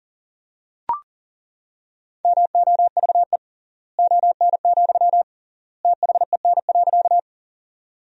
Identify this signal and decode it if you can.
Morse code, heard as a keyed tone, about 40 wpm, 700 Hz. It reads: MOVE ON, THEN.
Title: 3x Word-Spacing - Sentences from Top 200 Words - 40wpm